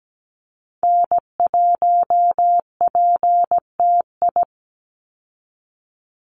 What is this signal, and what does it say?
Morse code, heard as a keyed tone, about 17 wpm, 700 Hz. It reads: N1PTI
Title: USA Call Signs 17wpm